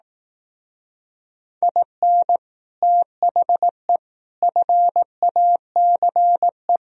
Morse code, heard as a keyed tone, about 18 wpm, 700 Hz. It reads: IN THE FACE
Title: Speed-Racing - Sets of 4 Words 15wpm